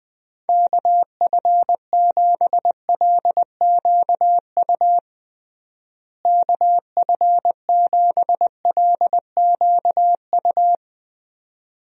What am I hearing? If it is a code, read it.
Morse code, heard as a keyed tone, about 21 wpm, 700 Hz. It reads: KF7LQU KF7LQU
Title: USA Call Signs - 6 Characters - Repeated 2x - 20wpm